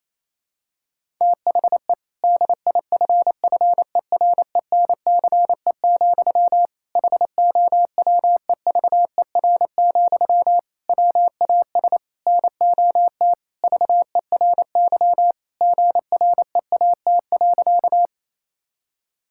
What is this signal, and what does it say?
Morse code, heard as a keyed tone, about 27 wpm, 700 Hz. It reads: THE DIFFERENCE, HOWEVER, WAS NOT VERY GREAT.
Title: Sentences from Top 1000 Words - Review Part 5 - 28wpm